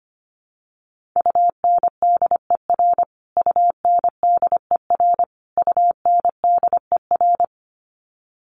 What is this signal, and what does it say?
Morse code, heard as a keyed tone, about 25 wpm, 700 Hz. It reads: UNDER UNDER UNDER
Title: Top 200 Words Repeated 25wpm